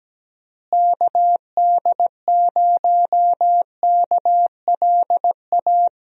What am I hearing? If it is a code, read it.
Morse code, heard as a keyed tone, about 17 wpm, 700 Hz. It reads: KD0KLA
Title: USA Call Signs - 6 Characters - Repeated 3x - 17wpm